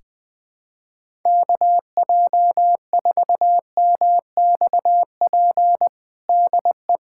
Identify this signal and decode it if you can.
Morse code, heard as a keyed tone, about 20 wpm, 700 Hz. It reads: KJ4MXP DE
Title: USA Callsign DE Callsign 20wpm